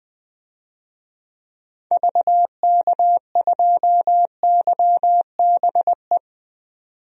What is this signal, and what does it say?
Morse code, heard as a keyed tone, about 20 wpm, 700 Hz. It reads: VK2YBE
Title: DXCC-100 Call Signs - 6 Characters - 20wpm